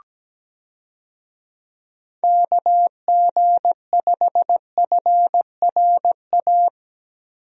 Morse code, heard as a keyed tone, about 17 wpm, 700 Hz. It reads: KG5FRA
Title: USA Call Signs - 6 Characters - 17wpm